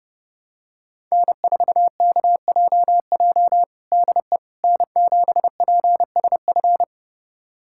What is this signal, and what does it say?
Morse code, heard as a keyed tone, about 30 wpm, 700 Hz. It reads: N4KJJ DE N7PSF
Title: USA Callsign DE Callsign - 5 Characters - 30wpm